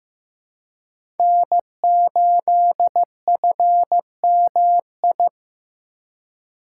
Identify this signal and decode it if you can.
Morse code, heard as a keyed tone, about 15 wpm, 700 Hz. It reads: N8FMI
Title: USA Call Signs - 5 Characters - 4x Character Spacing - 15wpm